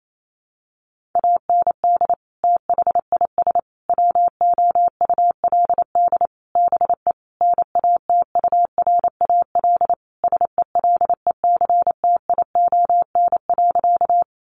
Morse code, heard as a keyed tone, about 28 wpm, 700 Hz. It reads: AND THIS WOULD BE NATURAL SELECTION.
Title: Sentences from Top 1400 Words 28wpm